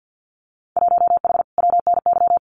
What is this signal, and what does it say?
Morse code, heard as a keyed tone, about 50 wpm, 700 Hz. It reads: 1H WNY
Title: Winter Field Day Exchanges 50wpm